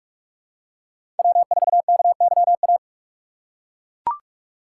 Morse code, heard as a keyed tone, about 42 wpm, 700 Hz. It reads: W4KYA
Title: USA Call Signs - 6x Character Spacing - 45wpm